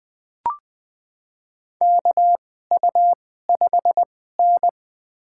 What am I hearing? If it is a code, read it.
Morse code, heard as a keyed tone, about 20 wpm, spaced as for 14 wpm, 700 Hz. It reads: KU5N
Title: USA Call Signs - 4 Characters - 2x Character Spacing - 20wpm